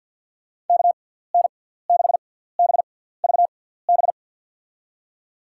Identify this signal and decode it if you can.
Morse code, heard as a keyed tone, about 52 wpm, spaced as for 17 wpm, 700 Hz. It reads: KN6BVB
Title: USA Call Signs - 6x Character Spacing - 50wpm